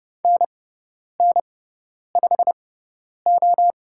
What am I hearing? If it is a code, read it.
Morse code, heard as a keyed tone, about 30 wpm, spaced as for 10 wpm, 700 Hz. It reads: NN5O
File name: Lesson 16g - Letter c - Call Signs 30wpm